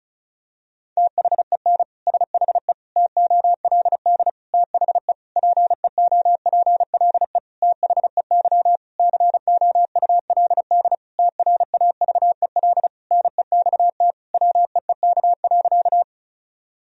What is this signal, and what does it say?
Morse code, heard as a keyed tone, about 34 wpm, 700 Hz. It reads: THEN SHE TOLD THE PEOPLE THEY COULD TRAVEL NEXT WEEK.
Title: Sentences from Top 1000 Words - Review Part 2 - 35wpm